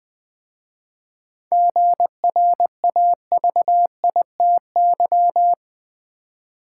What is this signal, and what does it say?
Morse code, heard as a keyed tone, about 20 wpm, 700 Hz. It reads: GRAVITY
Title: 7 Letter Words - First 3 Letters Given - 20wpm